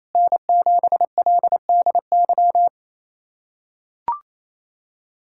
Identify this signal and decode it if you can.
Morse code, heard as a keyed tone, about 28 wpm, 700 Hz. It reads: N7LDY
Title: USA Callsign DE Callsign 28wpm